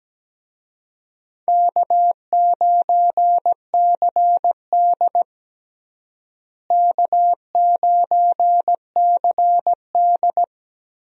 Morse code, heard as a keyed tone, about 17 wpm, 700 Hz. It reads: K9CD K9CD
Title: USA Call Signs - 4 Characters - Repeated 2x - 17wpm